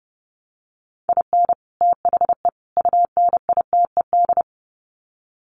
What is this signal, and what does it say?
Morse code, heard as a keyed tone, about 30 wpm, 700 Hz. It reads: IN THE UNITED
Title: Sets of 3 Words 30wpm